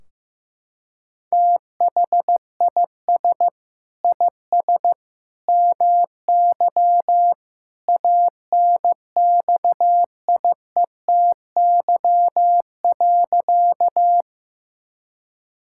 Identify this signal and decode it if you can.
Morse code, heard as a keyed tone, about 16 wpm, 700 Hz. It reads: THIS IS MY ANXIETY.
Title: Sentences from Top 1600 Words 15wpm